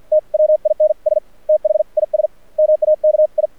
Morse code, heard as a keyed tone, about 50 wpm, spaced as for 42 wpm, 610 Hz. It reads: TWINS THIS MAKI
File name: archived / CRSnet